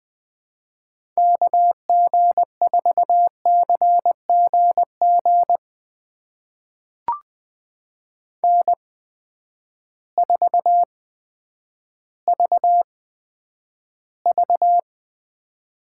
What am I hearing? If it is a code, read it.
Morse code, heard as a keyed tone, about 20 wpm, spaced as for 10 wpm, 700 Hz. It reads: KG4CGG N4VV
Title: USA Call Signs - 8x Character Spacing - 20wpm